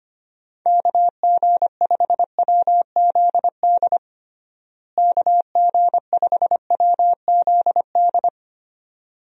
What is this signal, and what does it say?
Morse code, heard as a keyed tone, about 25 wpm, 700 Hz. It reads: KG5WZD KG5WZD
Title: USA Call Signs - 6 Characters - Repeated 2x - 25wpm